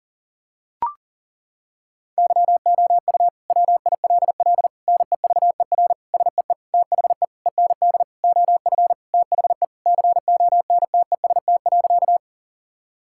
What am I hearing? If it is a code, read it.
Morse code, heard as a keyed tone, about 40 wpm, 700 Hz. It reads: YOU WILL NEVER SEE THE END OF THE CONTEST.
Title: Sentences from Top 1600 Words 40wpm